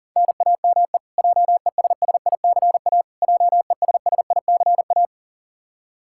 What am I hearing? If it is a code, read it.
Morse code, heard as a keyed tone, about 37 wpm, 700 Hz. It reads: NAME JESSICA JESSICA
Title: Practice QSO - Exchange 1 - Beginning - 40wpm